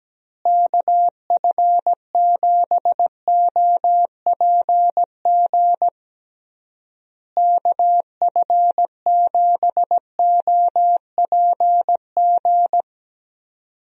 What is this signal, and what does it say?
Morse code, heard as a keyed tone, about 17 wpm, 700 Hz. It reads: KF7OPG KF7OPG
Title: USA Call Signs - 6 Characters - Repeated 3x - 17wpm